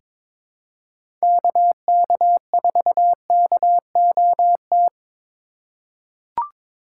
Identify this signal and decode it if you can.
Morse code, heard as a keyed tone, about 22 wpm, 700 Hz. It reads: KK4KOT
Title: USA Call Signs - 2x Character Spacing - 22wpm